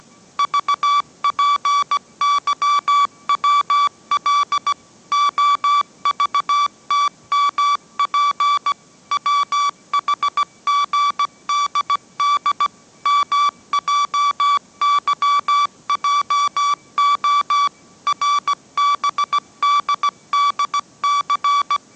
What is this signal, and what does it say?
Morse code, heard as a keyed tone, about 17 wpm, 1150 Hz. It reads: VPYWL OVTMP WHGDD MJYJO RBDDC
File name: mrx / mrx080 18wpm